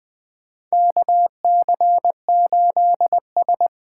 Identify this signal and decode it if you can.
Morse code, heard as a keyed tone, about 20 wpm, 700 Hz. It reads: KC8S
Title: USA Call Signs - 6 Characters - 8x Character Spacing - 20wpm